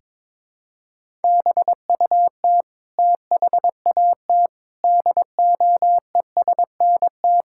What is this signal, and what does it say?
Morse code, heard as a keyed tone, about 22 wpm, 700 Hz. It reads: BUT THAT DOESNT
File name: Sentences with 7-Word Phrases - 22wpm